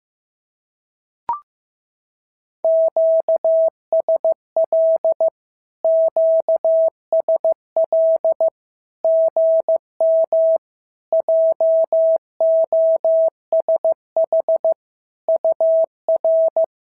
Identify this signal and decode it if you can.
Morse code, heard as a keyed tone, about 15 wpm, 650 Hz. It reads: QSL QSL GM JOSH UR
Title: SOTA QSO Practice 15wpm